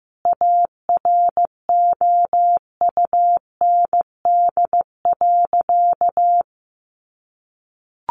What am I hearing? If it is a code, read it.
Morse code, heard as a keyed tone, about 15 wpm, 700 Hz. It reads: AROUND.
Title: Sentences from Top 600 Words 15wpm